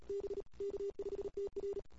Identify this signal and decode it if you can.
Morse code, heard as a keyed tone, about 34 wpm, 410 Hz. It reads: B K5TR
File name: CW / 2008-ss-cw-k5tr-op-n5rz -23:00